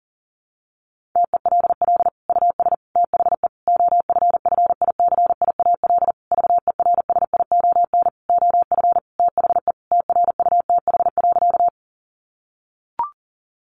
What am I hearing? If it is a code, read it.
Morse code, heard as a keyed tone, about 40 wpm, 700 Hz. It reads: TELL US THE OFFICIAL VERSION OF THE TRUTH.
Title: Sentences from Top 1400 Words 40wpm